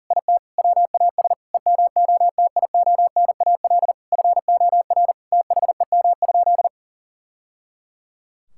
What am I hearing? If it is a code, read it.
Morse code, heard as a keyed tone, about 40 wpm, 700 Hz. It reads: IT WAS EMOTIONAL FOR THEM?
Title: Sentences from Top 1000 Words - Review Part 5 - 40wpm